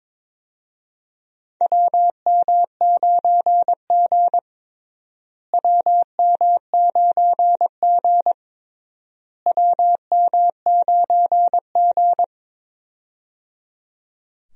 Morse code, heard as a keyed tone, about 22 wpm, 700 Hz. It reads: WM9G WM9G WM9G
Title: USA Call Signs - 4 Characters - Repeated 3x - 22wpm